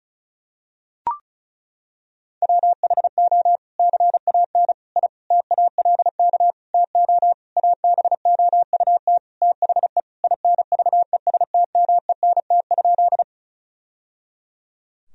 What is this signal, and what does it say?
Morse code, heard as a keyed tone, about 36 wpm, 700 Hz. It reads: WHO CAN I TALK TO ABOUT THE INVESTMENT?
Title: Sentences from Top 1000 Words - Review Part 4 - 35wpm